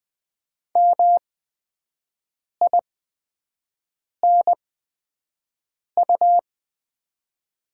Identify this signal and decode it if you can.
Morse code, heard as a keyed tone, about 20 wpm, spaced as for 5 wpm, 700 Hz. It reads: MINU
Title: 8x Character Spacing - Top 500 Words - 20wpm